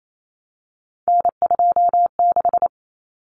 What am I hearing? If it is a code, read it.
Morse code, heard as a keyed tone, about 28 wpm, 700 Hz. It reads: N26
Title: Common Cut Numbers - 3-Digit 28wpm